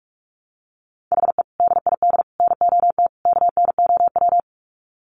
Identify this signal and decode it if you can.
Morse code, heard as a keyed tone, about 44 wpm, 700 Hz. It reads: HE DID NOT KNOW
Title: Sets of 4 Words 45wpm